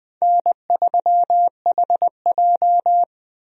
Morse code, heard as a keyed tone, about 21 wpm, 700 Hz. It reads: N3HJ
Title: USA Call Signs - 4 Characters - 20wpm